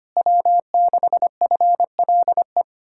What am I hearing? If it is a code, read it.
Morse code, heard as a keyed tone, about 30 wpm, 700 Hz. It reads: W6FLE
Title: Speed-Racing Common QSO Phrases Encore - 25wpm